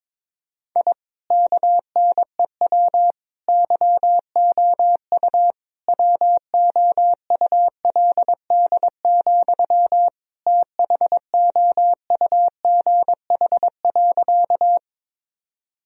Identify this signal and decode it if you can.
Morse code, heard as a keyed tone, about 22 wpm, 700 Hz. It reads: I KNEW YOU WOULD, THOUGH.
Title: Sentences from Top 1000 Words - Review Part 2 - 22wpm